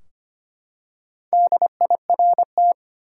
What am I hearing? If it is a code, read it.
Morse code, heard as a keyed tone, about 25 wpm, 700 Hz. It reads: DIRT
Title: Four Letter Words - 25wpm